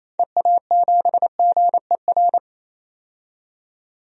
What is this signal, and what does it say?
Morse code, heard as a keyed tone, about 27 wpm, 700 Hz. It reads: EA7GER
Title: DXCC-100 Call Signs - 6 Characters - 28wpm